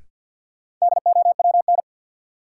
Morse code, heard as a keyed tone, about 53 wpm, 700 Hz. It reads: DOWN
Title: Sentences from Top 300 Words - New Word Drill - 50wpm